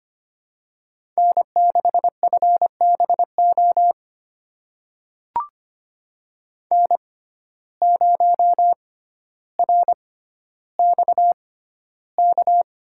Morse code, heard as a keyed tone, about 25 wpm, spaced as for 13 wpm, 700 Hz. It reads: N6FBO N0RXK
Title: USA Call Signs - 5 Characters - 6x Character Spacing - 25wpm